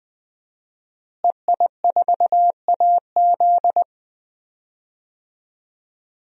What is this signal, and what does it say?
Morse code, heard as a keyed tone, about 21 wpm, 700 Hz. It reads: EI4AZ
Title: DXCC-100 Call Signs - 5 Characters - Repeated 2x - 20wpm